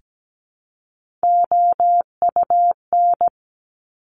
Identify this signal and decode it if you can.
Morse code, heard as a keyed tone, about 17 wpm, 690 Hz. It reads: OUN
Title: Top Three Letter Combinations in Words 17wpm